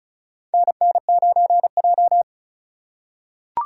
Morse code, heard as a keyed tone, about 36 wpm, 700 Hz. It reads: NN9J
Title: USA Call Signs - 4 Characters - 4x Character Spacing - 35wpm